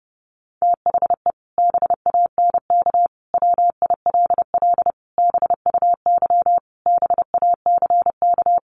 Morse code, heard as a keyed tone, about 30 wpm, 700 Hz. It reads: THE BANK WILL BUY BACK
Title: Sentences from Top 1500 Words 30wpm